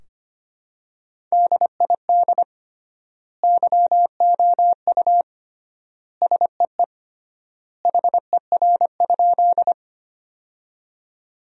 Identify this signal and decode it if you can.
Morse code, heard as a keyed tone, about 25 wpm, 700 Hz. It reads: DID YOU SEE HER?
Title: 3x Word-Spacing - Sentences from Top 100 Words - 25wpm